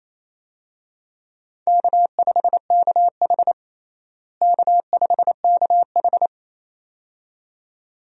Morse code, heard as a keyed tone, about 28 wpm, 700 Hz. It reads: K5KH K5KH
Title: USA Call Signs - 4 Characters - Repeated 2x - 28wpm